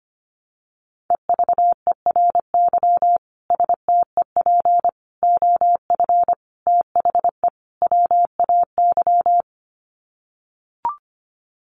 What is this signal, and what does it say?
Morse code, heard as a keyed tone, about 25 wpm, 700 Hz. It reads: EVERY STEP OF THE WAY
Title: Sets of 5 Words Encore2 25wpm